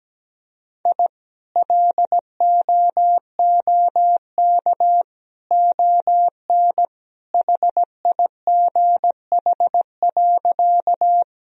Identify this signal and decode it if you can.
Morse code, heard as a keyed tone, about 17 wpm, 700 Hz. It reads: I LOOK ON HIGH.
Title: Sentences from Top 1000 Words - Review Part 1 - 17wpm